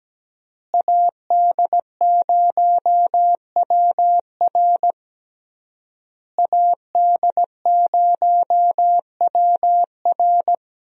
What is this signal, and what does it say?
Morse code, heard as a keyed tone, about 17 wpm, 700 Hz. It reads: AD0WR AD0WR
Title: USA Call Signs - 5 Characters - Repeated 3x - 17wpm